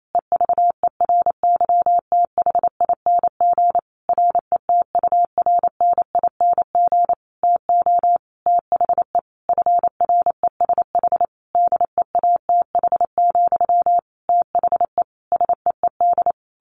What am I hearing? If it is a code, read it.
Morse code, heard as a keyed tone, about 28 wpm, 700 Hz. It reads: EVERYTHING RETURNING TO THE FRESH DEATH, THE SEED
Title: A Sound Of Thunder 28wpm